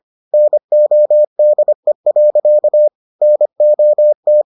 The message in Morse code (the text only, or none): NODE. NOT